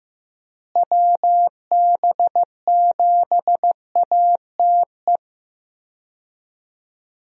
Morse code, heard as a keyed tone, about 15 wpm, 700 Hz. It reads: WB7ATE
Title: USA Call Signs - 6 Characters - Repeated 2x - 15wpm